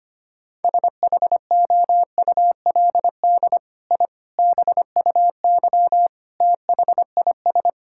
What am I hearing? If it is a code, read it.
Morse code, heard as a keyed tone, about 25 wpm, 700 Hz. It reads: SHOULD I BUY THIS
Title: Sentences from Top 1000 Words - Review Part 5 - 25wpm